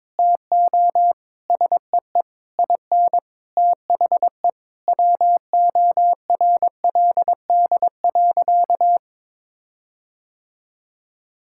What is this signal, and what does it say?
Morse code, heard as a keyed tone, about 22 wpm, 700 Hz. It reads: TO SEE IN THE WORLD.
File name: Sentences with 9-Word Phrases - 22wpm